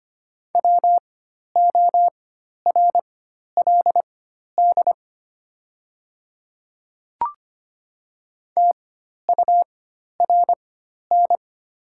Morse code, heard as a keyed tone, about 25 wpm, spaced as for 12 wpm, 700 Hz. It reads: WORLD TURN
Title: 4x Character Spacing - Top 300 Words - 25wpm